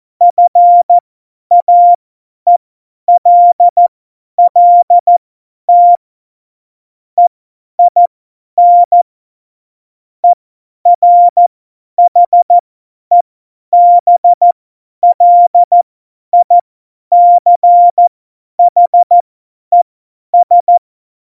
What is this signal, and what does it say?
Morse code, heard as a keyed tone, about 14 wpm, spaced as for 10 wpm, 700 Hz. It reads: FAELLT EIN ERHEBLICHES
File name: Morsekurs / Witze / Witz 14 Juli 2020, 50 F70